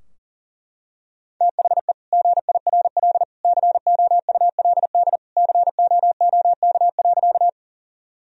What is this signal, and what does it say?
Morse code, heard as a keyed tone, about 40 wpm, 700 Hz. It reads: THE GIRL COULD COOK.
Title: Sentences from Top 1000 Words - Review Part 4 - 40wpm